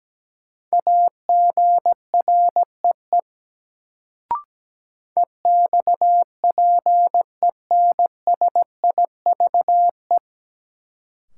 Morse code, heard as a keyed tone, about 17 wpm, 700 Hz. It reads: AGREE EXPENSIVE
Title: Sentences from Top 1000 Words - Review Part 6 - New Word Drill - 17wpm